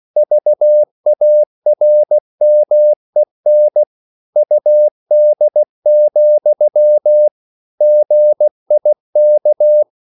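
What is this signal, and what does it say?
Morse code, heard as a keyed tone, about 16 wpm, 600 Hz. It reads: VARMEN UD, GIK
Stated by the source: Slow (QRS) Morse code at 16 wpm, from CW med Gnister.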